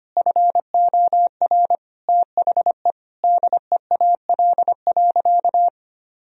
Morse code, heard as a keyed tone, about 25 wpm, 700 Hz. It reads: FOR THE DEAL.
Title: Sentences from Top 1000 Words - Review Part 4 - 25wpm